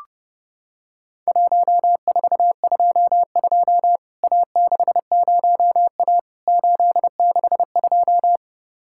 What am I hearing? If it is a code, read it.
Morse code, heard as a keyed tone, about 30 wpm, 700 Hz. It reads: 1422 A60A 862
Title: Worked All Europe DX Contest - QTC Exchange - 30wpm